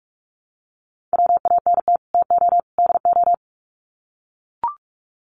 Morse code, heard as a keyed tone, about 44 wpm, 700 Hz. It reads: WANT TO DO
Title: Sets of 3 Words 45wpm